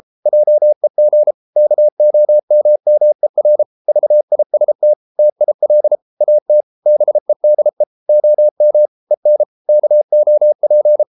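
Morse code, heard as a keyed tone, about 33 wpm, 600 Hz. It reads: JEG KOMMER VIST TIL AT BEDE OM EN KOP
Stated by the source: Morse code at 33 wpm, QRQ.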